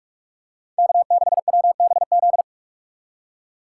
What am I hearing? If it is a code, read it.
Morse code, heard as a keyed tone, about 46 wpm, 700 Hz. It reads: K6WBZ